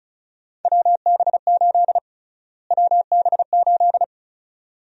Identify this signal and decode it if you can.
Morse code, heard as a keyed tone, about 36 wpm, 700 Hz. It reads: WB8 WB8
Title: USA Call Sign Prefixes - Repeated 2x - 35wpm